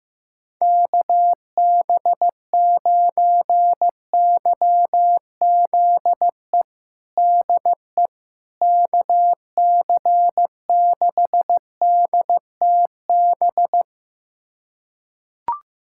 Morse code, heard as a keyed tone, about 15 wpm, 700 Hz. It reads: KB9YZE DE KC6DTB
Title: USA Callsign DE Callsign - 6 Characters - 15wpm